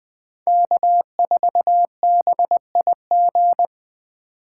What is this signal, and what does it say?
Morse code, heard as a keyed tone, about 21 wpm, 700 Hz. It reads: K4BIG
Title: USA Call Signs - 5 Characters - 3x Character Spacing - 20wpm